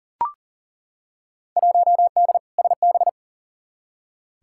Morse code, heard as a keyed tone, about 40 wpm, 700 Hz. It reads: 1D SB
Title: Field Day Exchanges - 40wpm